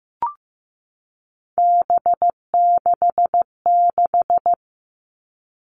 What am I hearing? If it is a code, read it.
Morse code, heard as a keyed tone, about 15 wpm, 700 Hz. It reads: B66
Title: All Cut Numbers - 3-Digit 15wpm